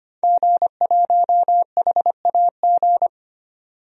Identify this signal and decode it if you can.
Morse code, heard as a keyed tone, about 25 wpm, 700 Hz. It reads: G1HAG
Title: DXCC-100 Call Signs - 5 Characters - 25wpm